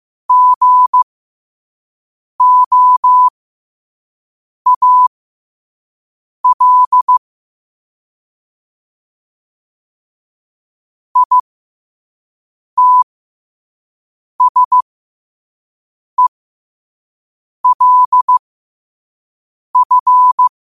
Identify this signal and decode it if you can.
Morse code, heard as a keyed tone, about 15 wpm, spaced as for 5 wpm, 1000 Hz. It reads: GOAL ITSELF
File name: qotdfiles / 5wpm